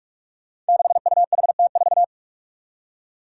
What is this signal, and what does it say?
Morse code, heard as a keyed tone, about 46 wpm, 700 Hz. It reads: 6UHT4
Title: Five Challenging Letter-Number Combinations 45wpm